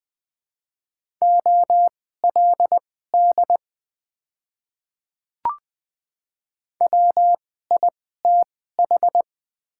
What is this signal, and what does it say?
Morse code, heard as a keyed tone, about 20 wpm, spaced as for 14 wpm, 700 Hz. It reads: OLD WITH
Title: 2x Character Spacing - Top 200 Words - 20wpm